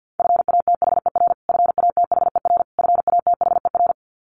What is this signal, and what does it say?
Morse code, heard as a keyed tone, about 50 wpm, 700 Hz. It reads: FATHER FATHER FATHER
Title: Top 500 Words Repeated 50wpm